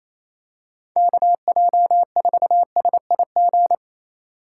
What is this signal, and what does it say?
Morse code, heard as a keyed tone, about 28 wpm, 700 Hz. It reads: KJ4SIG
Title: USA Call Signs - 6 Characters - 3x Character Spacing - 28wpm